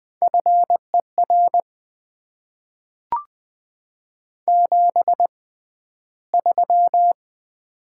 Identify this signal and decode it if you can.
Morse code, heard as a keyed tone, about 20 wpm, spaced as for 12 wpm, 700 Hz. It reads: FER 73
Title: Top 90 QSO Elements - 6x Character Spacing - 20wpm